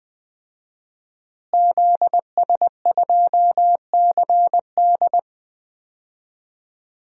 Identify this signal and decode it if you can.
Morse code, heard as a keyed tone, about 20 wpm, 700 Hz. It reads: ZS2CD
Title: DXCC-100 Call Signs - 5 Characters - Repeated 2x - 20wpm